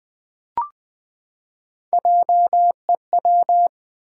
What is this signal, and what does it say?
Morse code, heard as a keyed tone, about 20 wpm, 700 Hz. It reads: JEW
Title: Build A Word - 3 Letters At A Time - 9-Letter Words - 20wpm